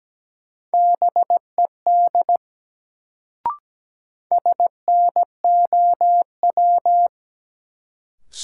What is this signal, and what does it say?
Morse code, heard as a keyed tone, about 17 wpm, 700 Hz. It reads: BED SNOW
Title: Sentences from Top 500 Words - New Word Drill - 17wpm